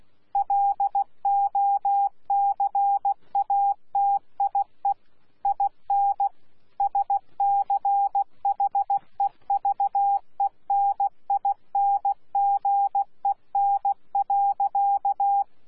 Morse code, heard as a keyed tone, about 16 wpm, 800 Hz. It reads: LOCATIE IN SCHEVENINGEN.
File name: Z PI4VRZ / PI4VRZ 14-jun-2025